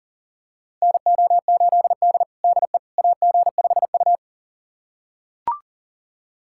Morse code, heard as a keyed tone, about 40 wpm, 700 Hz. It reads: NO8D DE AG5U
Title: USA Callsign DE Callsign - 4 Characters - 40wpm